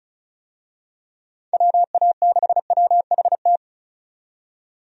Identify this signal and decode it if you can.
Morse code, heard as a keyed tone, about 36 wpm, 700 Hz. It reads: WA6WHT
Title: USA Call Signs - 6 Characters - 35wpm